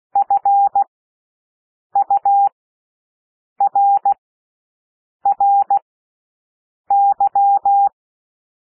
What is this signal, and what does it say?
Morse code, heard as a keyed tone, about 16 wpm, spaced as for 6 wpm, 800 Hz. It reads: FURRY
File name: cwmp3 / W1 SOUND WORDS 16 wpm